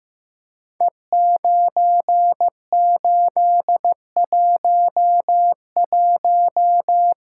Morse code, heard as a keyed tone, about 15 wpm, 700 Hz. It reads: E9811
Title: Code Groups - 5 Letter:Numbers - 15wpm